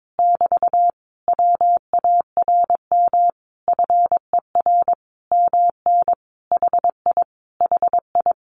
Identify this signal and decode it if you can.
Morse code, heard as a keyed tone, about 22 wpm, 700 Hz. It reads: <BT> WARM FER MN HI HI
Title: Common QSO Phrases Encore 22wpm